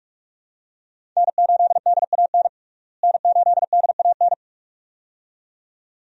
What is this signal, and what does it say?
Morse code, heard as a keyed tone, about 47 wpm, 700 Hz. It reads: N8DAN N8DAN
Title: USA Call Signs - 5 Characters - Repeated 2x - 45wpm